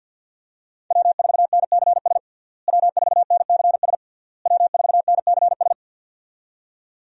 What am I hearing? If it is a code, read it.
Morse code, heard as a keyed tone, about 48 wpm, 700 Hz. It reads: W4NCS W4NCS W4NCS
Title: USA Call Signs - 5 Characters - Repeated 3x - 50wpm